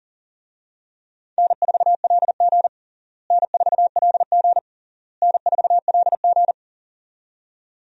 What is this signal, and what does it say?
Morse code, heard as a keyed tone, about 40 wpm, 700 Hz. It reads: N4LG N4LG N4LG
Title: USA Call Signs - 4 Characters - Repeated 3x - 40wpm